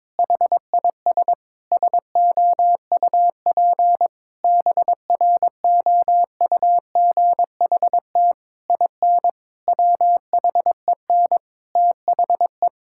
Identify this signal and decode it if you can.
Morse code, heard as a keyed tone, about 23 wpm, 700 Hz. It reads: HIS SOUP BROUGHT IN WHEN THE
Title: Sentences from Top 1000 Words - Review Part 6 - 22wpm